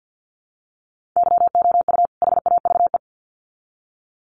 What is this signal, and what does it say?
Morse code, heard as a keyed tone, about 50 wpm, 700 Hz. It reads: YOU HAVE
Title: Sets of 2 Words 50wpm